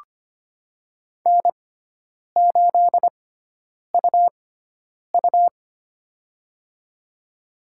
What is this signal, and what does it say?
Morse code, heard as a keyed tone, about 25 wpm, spaced as for 9 wpm, 700 Hz. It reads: N8UU
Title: USA Call Signs - 4 Characters - 6x Character Spacing - 25wpm